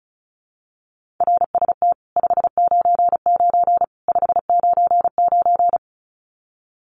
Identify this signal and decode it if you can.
Morse code, heard as a keyed tone, about 34 wpm, 700 Hz. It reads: RST 599 599
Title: Common QSO Phrases 35wpm